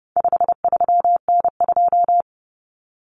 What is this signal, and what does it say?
Morse code, heard as a keyed tone, about 30 wpm, 700 Hz. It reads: 53N2
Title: Common Cut Numbers - 4-Digit 30wpm